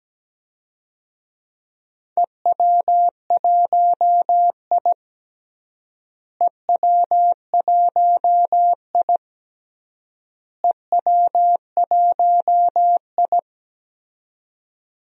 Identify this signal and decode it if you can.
Morse code, heard as a keyed tone, about 17 wpm, 700 Hz. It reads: EW1I EW1I EW1I
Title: DXCC-100 Call Signs - 4 Characters - Repeated 3x - 17wpm